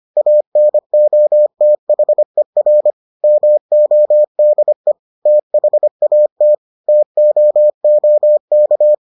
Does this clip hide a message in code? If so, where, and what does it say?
Yes: Morse code from 0.2 to 8.9 s, heard as a keyed tone, about 25 wpm, 600 Hz. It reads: ANOTHER MODE THAT TOOK